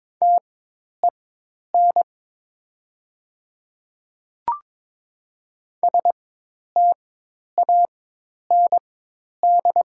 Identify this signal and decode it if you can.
Morse code, heard as a keyed tone, about 22 wpm, spaced as for 10 wpm, 700 Hz. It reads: TEN STAND
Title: 4x Character Spacing - Top 500 Words - 22wpm